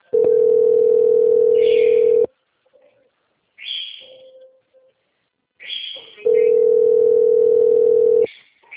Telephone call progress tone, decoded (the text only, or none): ringback tone